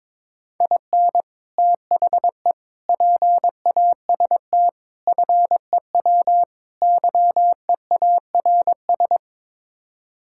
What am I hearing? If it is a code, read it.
Morse code, heard as a keyed tone, about 22 wpm, 700 Hz. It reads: IN THE PAST FEW YEARS
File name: practice / 22wpm / Build A Phrase - 5-Word Phrases - 22wpm